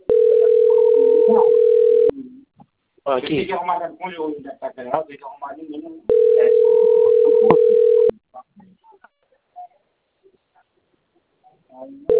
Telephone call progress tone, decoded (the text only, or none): ringback tone